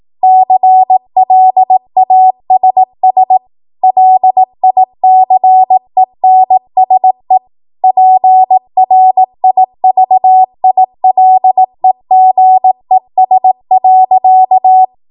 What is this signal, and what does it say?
Morse code, heard as a keyed tone, about 18 wpm, 750 Hz. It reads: CLASS LICENSE PRIVILEGES.
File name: Morse / 18WPM